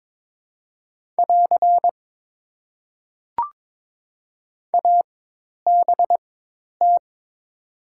Morse code, heard as a keyed tone, about 22 wpm, spaced as for 10 wpm, 690 Hz. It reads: <AR> ABT
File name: Top 90 QSO Elements - 4x Character Spacing - 22wpm